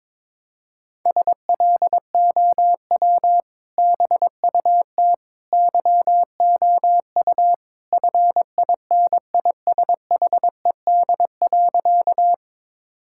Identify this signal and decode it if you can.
Morse code, heard as a keyed tone, about 22 wpm, 700 Hz. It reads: SLOW BUT YOU FINISHED.
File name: practice / 22wpm / Sentences with 2-Word Phrases - 22wpm